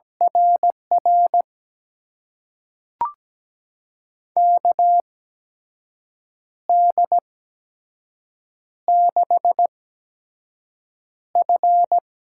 Morse code, heard as a keyed tone, about 17 wpm, spaced as for 6 wpm, 700 Hz. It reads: RR KD6F
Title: USA Call Signs - 8x Character Spacing - 17wpm